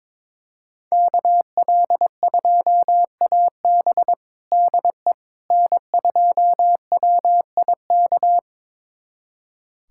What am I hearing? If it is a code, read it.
Morse code, heard as a keyed tone, about 21 wpm, 700 Hz. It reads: KL2AB DE N2WIK
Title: USA Callsign DE Callsign - 5 Characters - 22wpm